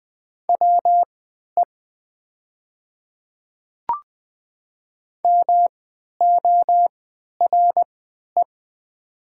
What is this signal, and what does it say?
Morse code, heard as a keyed tone, about 20 wpm, spaced as for 11 wpm, 700 Hz. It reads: WE MORE
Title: 3x Character Spacing - Top 100 Words - 20wpm